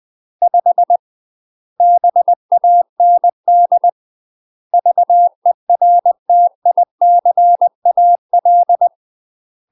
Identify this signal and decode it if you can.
Morse code, heard as a keyed tone, about 20 wpm, 700 Hz. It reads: 5 BAND VERTICAL
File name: UK MASTER QSOs TRAINING/20WPM